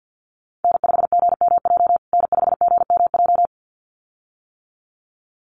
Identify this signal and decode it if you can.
Morse code, heard as a keyed tone, about 50 wpm, 700 Hz. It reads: N5GMJ N5GMJ
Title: USA Call Signs Repeated 2x 50wpm